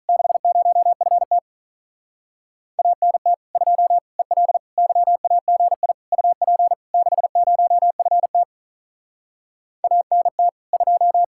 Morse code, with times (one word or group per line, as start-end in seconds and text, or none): 0.1-1.4 60FT
2.8-3.3 ANT
3.6-4.0 2
4.2-4.6 EL
4.8-5.9 YAGI
6.1-6.7 UP
6.9-8.4 60FT
9.8-10.5 ANT
10.7-11.2 2